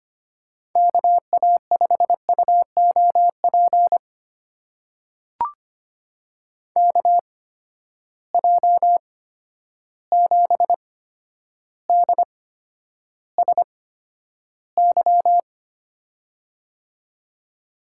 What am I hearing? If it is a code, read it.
Morse code, heard as a keyed tone, about 25 wpm, spaced as for 11 wpm, 700 Hz. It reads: KA5UOP KJ7DSY
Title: USA Call Signs - 6 Characters - 8x Character Spacing - 25wpm